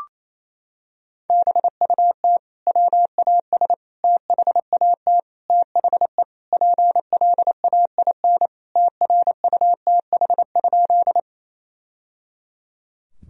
Morse code, heard as a keyed tone, about 28 wpm, 700 Hz. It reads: BUT WAS THAT THE PLAIN TRUTH?
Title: Sentences from Top 1000 Words - Review Part 2 - 28wpm